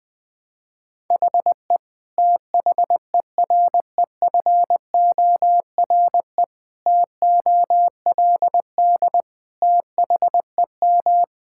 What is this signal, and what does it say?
Morse code, heard as a keyed tone, about 20 wpm, 700 Hz. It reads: HE THEREFORE TOLD THEM
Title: Sentences from Top 1000 Words - Review Part 5 - 20wpm